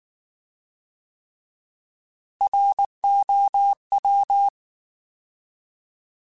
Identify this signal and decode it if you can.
Morse code, heard as a keyed tone, about 19 wpm, 800 Hz. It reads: ROW
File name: Alphabetical / Alphabetical mp3s / Rs @18